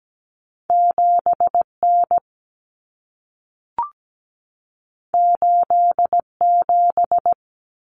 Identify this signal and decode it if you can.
Morse code, heard as a keyed tone, about 17 wpm, 700 Hz. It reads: 7N 87
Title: Common Cut Numbers - 2-Digit 17wpm